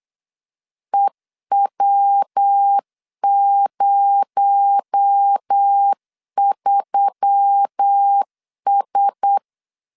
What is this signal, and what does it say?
Morse code, heard as a keyed tone, about 8 wpm, 800 Hz. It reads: EW03S
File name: LEZIONI / 6 PKX0 / 6 PKX0 8wpm 3341